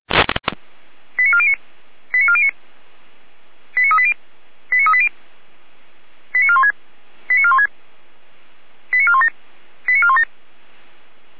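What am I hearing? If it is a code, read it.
Five-tone selective call (ZVEI), heard as quick runs of five tones, tones of about 70 ms. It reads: 89409 89409 89380 89380 89416 89416 89417 89417
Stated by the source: ZVEI five-tone call received by SDR.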